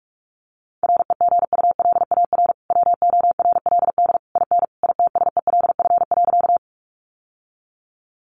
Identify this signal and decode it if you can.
Morse code, heard as a keyed tone, about 44 wpm, 700 Hz. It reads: REGULAR WORLD IN ITSELF.
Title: Sentences from Top 900 Words 45wpm